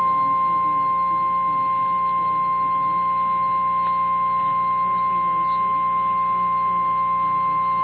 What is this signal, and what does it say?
Emergency Alert System (SAME) tones (the 1050 Hz attention tone). It is heard from 0.4 to 7.1 s.
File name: eas / KNCYKBIE / KNCYKBIE